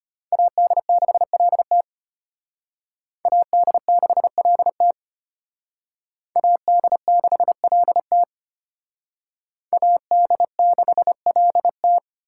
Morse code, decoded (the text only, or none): AD6LT AD6LT AD6LT AD6LT